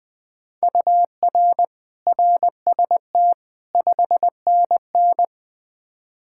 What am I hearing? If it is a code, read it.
Morse code, heard as a keyed tone, about 21 wpm, 700 Hz. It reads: UR RST 5NN
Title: K1USN SST QSO Practice 20wpm